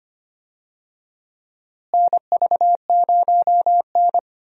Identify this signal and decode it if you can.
Morse code, heard as a keyed tone, about 25 wpm, 700 Hz. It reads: NV0N
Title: USA Call Signs - 4 Characters - Repeated 3x - 25wpm